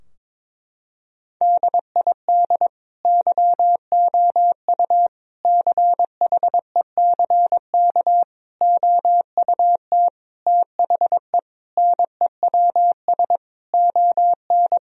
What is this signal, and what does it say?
Morse code, heard as a keyed tone, about 22 wpm, 700 Hz. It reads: DID YOU CHECK OUT THE NEWS ON
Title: Sentences from Top 1000 Words - Review Part 4 - 22wpm